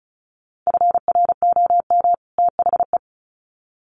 Morse code, heard as a keyed tone, about 34 wpm, 700 Hz. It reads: FROM THE
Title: Sets of 2 Words 35wpm